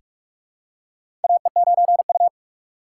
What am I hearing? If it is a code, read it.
Morse code, heard as a keyed tone, about 45 wpm, 700 Hz. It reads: AE9U
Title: USA Call Signs - 4 Characters - 6x Character Spacing - 45wpm